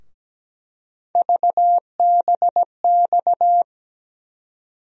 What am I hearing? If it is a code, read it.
Morse code, heard as a keyed tone, about 17 wpm, 700 Hz. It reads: VBX
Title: USA Call Sign Postfixes - 3 Letters 17wpm